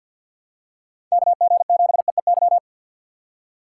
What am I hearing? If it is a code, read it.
Morse code, heard as a keyed tone, about 49 wpm, 700 Hz. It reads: KG7EEY
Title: USA Call Signs - 6 Characters - 4x Character Spacing - 50wpm